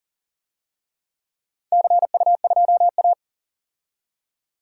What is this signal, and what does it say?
Morse code, heard as a keyed tone, about 40 wpm, 700 Hz. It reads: CU2A
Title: DXCC-100 Call Signs - 4 Characters - 40wpm